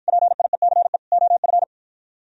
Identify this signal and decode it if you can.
Morse code, heard as a keyed tone, about 48 wpm, 700 Hz. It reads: PIECE OF